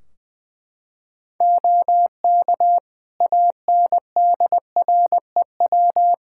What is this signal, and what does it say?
Morse code, heard as a keyed tone, about 20 wpm, 700 Hz. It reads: OK ANDREW
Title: Practice QSO - Exchange 3 - More Info - 20wpm